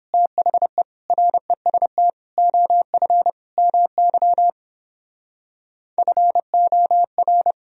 Morse code, heard as a keyed tone, about 31 wpm, 700 Hz. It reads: THE REST OF MY FOR